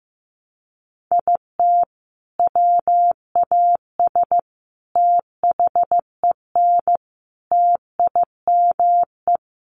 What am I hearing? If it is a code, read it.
Morse code, heard as a keyed tone, about 15 wpm, 700 Hz. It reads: IT WAS THEN TIME
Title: Sentences from Top 300 Words 15wpm